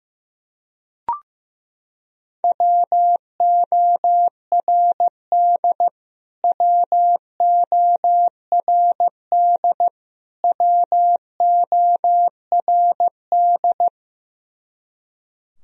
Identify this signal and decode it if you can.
Morse code, heard as a keyed tone, about 15 wpm, 700 Hz. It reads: WORD WORD WORD
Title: Top 100 Words Repeated 15wpm